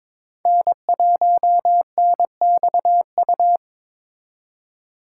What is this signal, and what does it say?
Morse code, heard as a keyed tone, about 26 wpm, 700 Hz. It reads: N1NXU
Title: Speed Racing - USA 1x3 Call Signs 22wpm